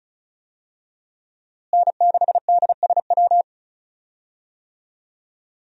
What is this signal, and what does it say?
Morse code, heard as a keyed tone, about 36 wpm, 700 Hz. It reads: N6DSW
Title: USA Call Signs - 5 Characters - Repeated 3x - 35wpm